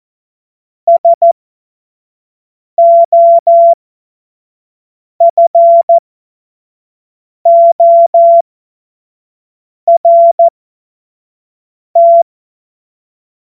Morse code, heard as a keyed tone, about 14 wpm, spaced as for 5 wpm, 680 Hz. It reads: SOFORT
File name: Morsekurs / Witze / Witz 02 Januar 2024 25 F70